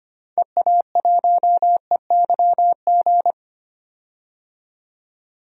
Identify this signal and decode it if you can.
Morse code, heard as a keyed tone, about 25 wpm, 700 Hz. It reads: EA1EYG
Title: DXCC-100 Call Signs - 6 Characters - 25wpm